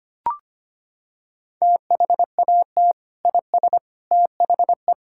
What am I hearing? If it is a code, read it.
Morse code, heard as a keyed tone, about 25 wpm, 700 Hz. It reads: THAT IS THE
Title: Sentences from Top 1000 Words - Review Part 5 - 25wpm